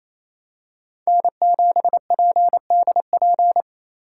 Morse code, heard as a keyed tone, about 28 wpm, 700 Hz. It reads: N7PDP
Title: USA Call Signs - 5 Characters - 4x Character Spacing - 28wpm